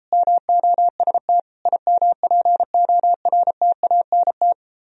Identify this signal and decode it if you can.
Morse code, heard as a keyed tone, about 33 wpm, 700 Hz. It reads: MOST IMPORTANT